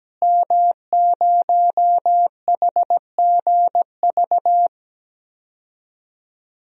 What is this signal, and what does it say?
Morse code, heard as a keyed tone, about 17 wpm, 700 Hz. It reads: M0HGV
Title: DXCC-100 Call Signs - 5 Characters - Repeated 3x - 17wpm